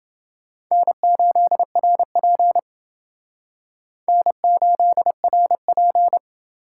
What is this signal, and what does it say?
Morse code, heard as a keyed tone, about 28 wpm, 700 Hz. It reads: N8RP N8RP